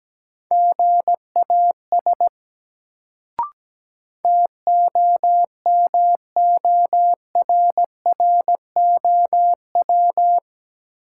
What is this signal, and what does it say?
Morse code, heard as a keyed tone, about 17 wpm, 700 Hz. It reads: GAS TOMORROW
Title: Sentences from Top 1100 Words - New Word Drill - 17wpm